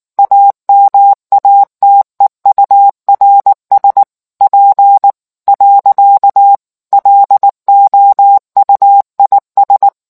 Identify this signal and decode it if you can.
Morse code, heard as a keyed tone, about 19 wpm, 800 Hz. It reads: AMATEURS P . LOUIS